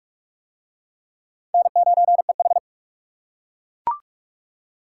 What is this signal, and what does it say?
Morse code, heard as a keyed tone, about 39 wpm, 700 Hz. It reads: N9EH